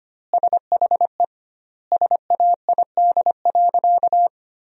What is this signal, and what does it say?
Morse code, heard as a keyed tone, about 25 wpm, 700 Hz. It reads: SHE SAID.
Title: 2x Word-Spacing - Sentences from Top 500 Words - 25wpm